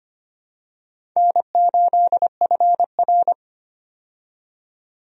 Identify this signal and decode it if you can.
Morse code, heard as a keyed tone, about 25 wpm, 700 Hz. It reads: N8FR
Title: USA Call Signs - 4 Characters - 25wpm